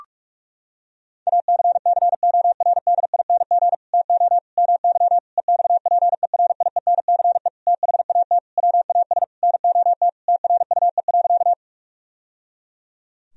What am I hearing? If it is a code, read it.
Morse code, heard as a keyed tone, about 47 wpm, 700 Hz. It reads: ACCORDING TO MY EXPERIENCE THAT WAS NOT TRUE.
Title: Sentences from Top 1000 Words - Review Part 4 - 45wpm